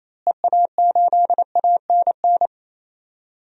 Morse code, heard as a keyed tone, about 28 wpm, 700 Hz. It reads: EA8ANN
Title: DXCC-100 Call Signs - 6 Characters - Repeated 2x - 28wpm